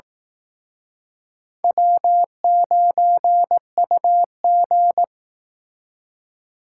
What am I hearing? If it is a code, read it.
Morse code, heard as a keyed tone, about 19 wpm, 700 Hz. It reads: W9UG